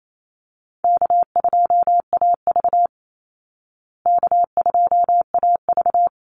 Morse code, heard as a keyed tone, about 28 wpm, 700 Hz. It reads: K2AV K2AV
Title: Easy Call Signs Repeated 3x - 28wpm